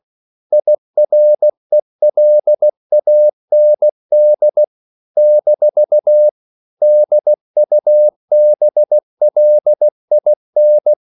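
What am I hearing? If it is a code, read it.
Morse code, heard as a keyed tone, about 16 wpm, 600 Hz. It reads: IRELAND - DUBLIN
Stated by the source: CW practice of national capitals at 16 words per minute, from ON6ZQ.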